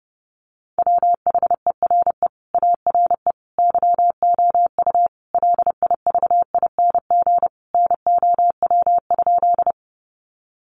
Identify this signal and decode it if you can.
Morse code, heard as a keyed tone, about 30 wpm, 700 Hz. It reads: WHERE ARE YOU LIVING NOW?
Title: Sentences from Top 1100 Words 30wpm